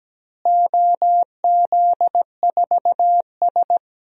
Morse code, heard as a keyed tone, about 17 wpm, 700 Hz. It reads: OZ4S
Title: DXCC-100 Call Signs - 4 Characters - Repeated 3x - 17wpm